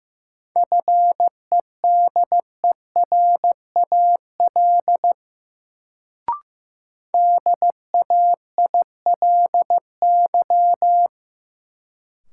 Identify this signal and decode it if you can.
Morse code, heard as a keyed tone, about 16 wpm, 700 Hz. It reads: FEDERAL DAILY
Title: Sentences from Top 1100 Words - New Word Drill Alt - 15wpm